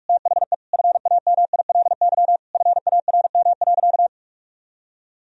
Morse code, heard as a keyed tone, about 46 wpm, 700 Hz. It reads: THE FAMILY FARM.
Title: Sentences from Top 1000 Words - Review Part 2 - 45wpm